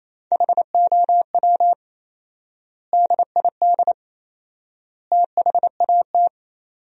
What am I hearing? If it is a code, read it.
Morse code, heard as a keyed tone, about 28 wpm, 700 Hz. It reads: HOW DID THAT
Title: 4x Word-Spacing - Sentences from Top 200 Words - 28wpm